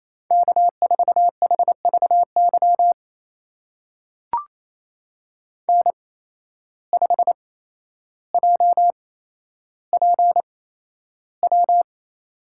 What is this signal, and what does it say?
Morse code, heard as a keyed tone, about 28 wpm, spaced as for 12 wpm, 700 Hz. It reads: K4HVY N5JPW
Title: USA Call Signs - 5 Characters - 8x Character Spacing - 28wpm